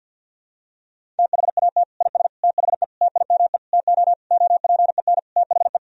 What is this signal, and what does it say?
Morse code, heard as a keyed tone, about 51 wpm, 700 Hz. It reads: THAT IS THE TIME TO OPEN THE
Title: Sentences from Top 1000 Words - Review Part 1 - 50wpm